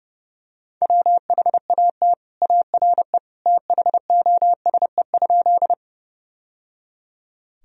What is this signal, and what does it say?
Morse code, heard as a keyed tone, about 30 wpm, 700 Hz. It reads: WHAT ARE THOSE?
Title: Sentences from Top 1000 Words - Review Part 1 - 30wpm